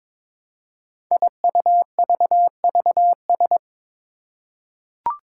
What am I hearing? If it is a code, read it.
Morse code, heard as a keyed tone, about 22 wpm, 700 Hz. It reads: IUVVS
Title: Five Challenging Letter-Number Combinations 22wpm